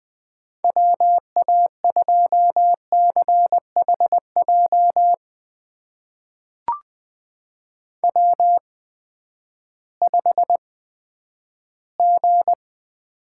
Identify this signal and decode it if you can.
Morse code, heard as a keyed tone, about 20 wpm, spaced as for 11 wpm, 700 Hz. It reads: WA2CHJ W5G
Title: USA Call Signs - 8x Character Spacing - 20wpm